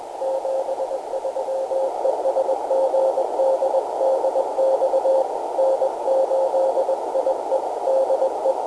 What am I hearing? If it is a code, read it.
Morse code, heard as a keyed tone, about 21 wpm, 530 Hz, with radio noise. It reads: 73 ES GD DX N8IE DE
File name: camano / g3fpq-topband